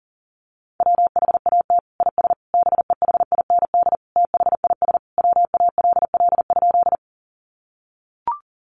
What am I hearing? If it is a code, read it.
Morse code, heard as a keyed tone, about 40 wpm, 700 Hz. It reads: WHAT IS BEHIND THIS WALL?
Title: Sentences from Top 1100 Words 40wpm